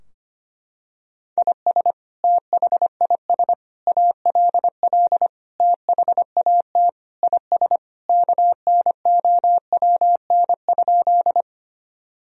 Morse code, heard as a keyed tone, about 25 wpm, 700 Hz. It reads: IS THIS ALL THAT IS KNOWN?
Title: Sentences from Top 1000 Words - Review Part 3 - 25wpm